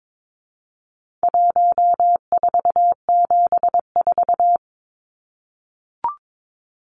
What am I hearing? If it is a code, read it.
Morse code, heard as a keyed tone, about 22 wpm, 700 Hz. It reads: 1474
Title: Numbers - 4-Digit 22wpm